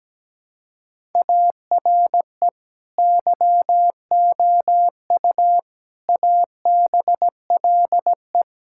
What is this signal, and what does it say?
Morse code, heard as a keyed tone, about 17 wpm, 700 Hz. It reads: ARE YOU ABLE
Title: Sentences from Top 1000 Words - Review Part 6 - 17wpm